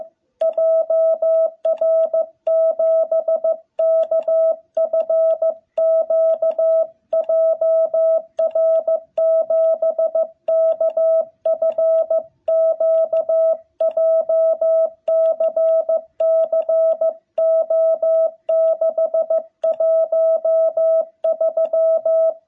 Morse code, heard as a keyed tone, about 15 wpm, 660 Hz. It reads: JR7KFQ JR7KFQ JCC O613
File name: jh8cbh / 2021cw